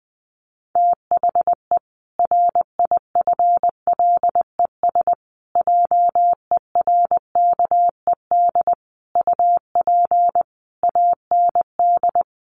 Morse code, heard as a keyed tone, about 20 wpm, 700 Hz. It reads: THE RIFLES JERKED UP AND
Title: A Sound Of Thunder 20wpm